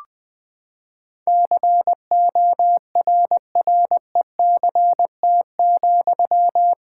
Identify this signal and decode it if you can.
Morse code, heard as a keyed tone, about 20 wpm, 700 Hz. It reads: CORRECT,
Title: Sentences from Top 1600 Words 20wpm